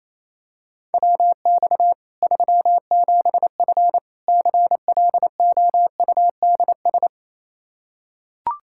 Morse code, heard as a keyed tone, about 28 wpm, 700 Hz. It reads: WX 37F CLOUDS
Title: Practice QSO - Exchange 2 - Info - 28wpm